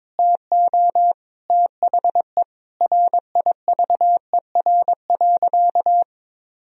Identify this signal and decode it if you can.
Morse code, heard as a keyed tone, about 22 wpm, 700 Hz. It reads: TO THE RIVER.
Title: Sentences from Top 1000 Words - Review Part 1 - 22wpm